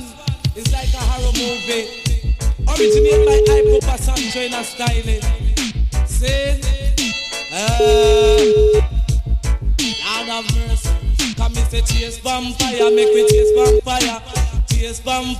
Ringback tone (Europe, 425 Hz), with noise.